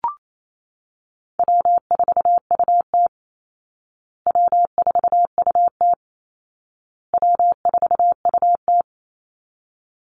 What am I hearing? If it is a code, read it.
Morse code, heard as a keyed tone, about 28 wpm, 700 Hz. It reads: W4UT W4UT W4UT
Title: Easy Call Signs Repeated 3x - 28wpm